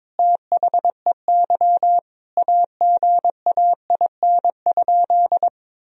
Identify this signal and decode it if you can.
Morse code, heard as a keyed tone, about 22 wpm, 700 Hz. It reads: THEY AGAIN?
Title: Sentences from Top 1000 Words - Review Part 3 - 22wpm